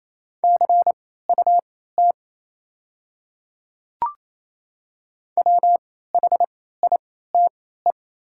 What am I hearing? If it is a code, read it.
Morse code, heard as a keyed tone, about 28 wpm, spaced as for 16 wpm, 700 Hz. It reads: CUT WHITE
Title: 3x Character Spacing - Top 300 Words - 28wpm